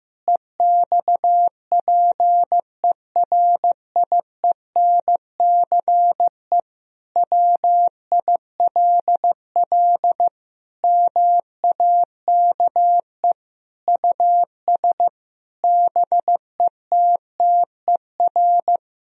Morse code, heard as a keyed tone, about 15 wpm, 700 Hz. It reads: EXPERIENCE WILL MAKE US BETTER
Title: Sentences from Top 1000 Words - Review Part 3 - 15wpm